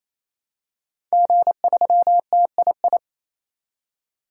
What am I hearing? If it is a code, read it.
Morse code, heard as a keyed tone, about 28 wpm, 700 Hz. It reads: G3TII
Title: DXCC-100 Call Signs - 5 Characters - 28wpm